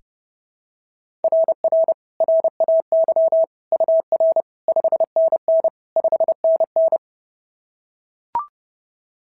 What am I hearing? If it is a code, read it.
Morse code, heard as a keyed tone, about 30 wpm, 690 Hz. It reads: RR RAY UR 5NN 5NN
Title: SOTA QSO Practice 30wpm